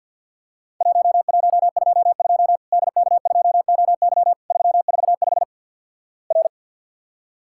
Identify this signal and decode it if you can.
Morse code, heard as a keyed tone, about 48 wpm, 700 Hz. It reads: 1122 DK2OY 345 R
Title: Worked All Europe DX Contest - QTC Exchange - 50wpm